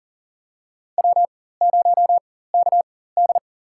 Morse code, heard as a keyed tone, about 40 wpm, spaced as for 19 wpm, 700 Hz. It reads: W0KD
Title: USA Call Signs - 4 Characters - 4x Character Spacing - 40wpm